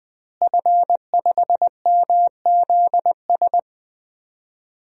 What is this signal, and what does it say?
Morse code, heard as a keyed tone, about 21 wpm, 700 Hz. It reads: F5MZS
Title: DXCC-100 Call Signs - 5 Characters - Repeated 3x - 20wpm